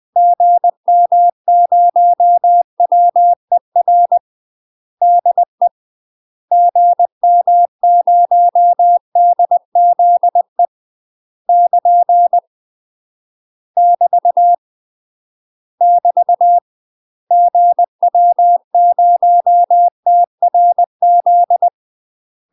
Morse code, heard as a keyed tone, about 20 wpm, 700 Hz. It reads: GM0WER DE GM0DZE <KN> <BT> <BT> GW0TRZ
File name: UK MASTER QSOs TRAINING / 20WPM / UK MASTER QSOs 0044